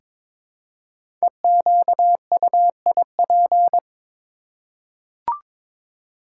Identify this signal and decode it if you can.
Morse code, heard as a keyed tone, about 22 wpm, 700 Hz. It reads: EQUIP
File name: 5 Letter Words - First 3 Letters Given - 22wpm